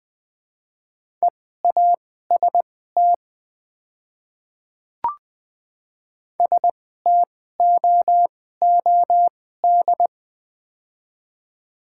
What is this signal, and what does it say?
Morse code, heard as a keyed tone, about 20 wpm, spaced as for 14 wpm, 690 Hz. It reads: EAST STOOD
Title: 2x Character Spacing - Top 500 Words - 20wpm